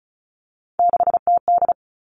Morse code, heard as a keyed tone, about 36 wpm, 700 Hz. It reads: 6TD
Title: Three Challenging Letter-Number Combinations 35wpm